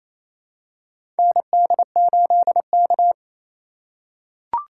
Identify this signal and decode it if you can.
Morse code, heard as a keyed tone, about 28 wpm, 700 Hz. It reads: ND8K
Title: USA Call Signs - 2x Character Spacing - 28wpm